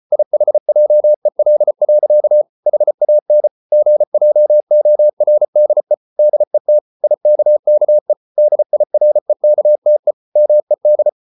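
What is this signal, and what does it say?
Morse code, heard as a keyed tone, about 34 wpm, 600 Hz. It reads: IHJEL. HAN GJORDE DET IKKE DIREKTE MED
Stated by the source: Morse code at 33 wpm, QRQ.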